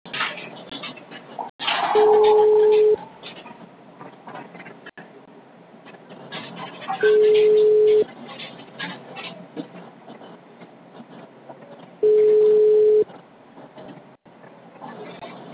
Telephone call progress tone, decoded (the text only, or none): ringback tone